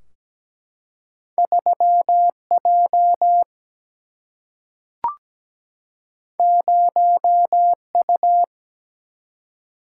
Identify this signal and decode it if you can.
Morse code, heard as a keyed tone, about 17 wpm, 700 Hz. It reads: 3 J 0 U
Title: Single-Letter Single-Number 17wpm